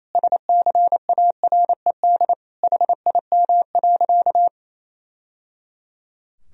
Morse code, heard as a keyed tone, about 28 wpm, 700 Hz. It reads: SCARED HIM.
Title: Sentences from Top 1000 Words - Review Part 4 - 28wpm